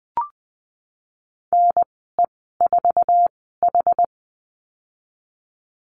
Morse code, heard as a keyed tone, about 20 wpm, spaced as for 14 wpm, 700 Hz. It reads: NE4H
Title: 088 Lesson 15g - Number 5 - Call Signs 20wpm